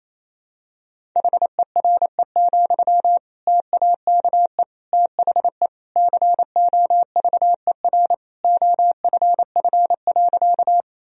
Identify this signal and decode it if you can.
Morse code, heard as a keyed tone, about 28 wpm, 700 Hz. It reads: HERE, TAKE THE COVER OFF.
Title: Sentences from Top 1000 Words - Review Part 1 - 28wpm